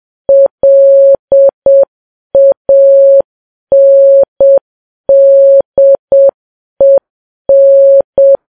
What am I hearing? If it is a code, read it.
Morse code, heard as a keyed tone, about 7 wpm, 550 Hz. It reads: LANDEN
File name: Files Web tls / Morse on9fv / morse7kps